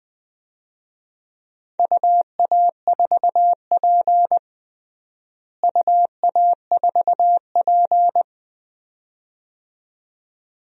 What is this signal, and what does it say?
Morse code, heard as a keyed tone, about 20 wpm, 700 Hz. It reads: UA4P UA4P
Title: DXCC-100 Call Signs - 4 Characters - Repeated 2x - 20wpm